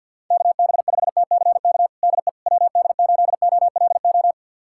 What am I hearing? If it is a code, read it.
Morse code, heard as a keyed tone, about 52 wpm, 700 Hz. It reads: KB5TCK DE WD8OLO
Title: USA Callsign DE Callsign - 6 Characters - 50wpm